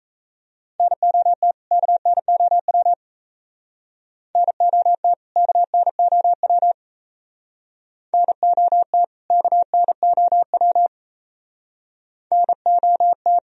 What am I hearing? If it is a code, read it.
Morse code, heard as a keyed tone, about 36 wpm, 700 Hz. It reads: NOT KNOW NOT KNOW NOT KNOW NOT